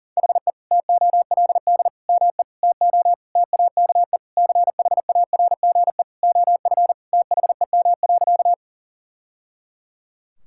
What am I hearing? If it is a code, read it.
Morse code, heard as a keyed tone, about 40 wpm, 700 Hz. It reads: HE TOLD ME TO TAKE CHARGE OF THEM.
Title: Sentences from Top 1000 Words - Review Part 6 - 40wpm